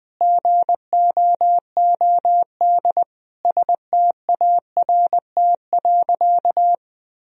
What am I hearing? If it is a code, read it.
Morse code, heard as a keyed tone, about 21 wpm, 700 Hz. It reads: GOOD START.
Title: Sentences from Top 1000 Words - Review Part 1 - 20wpm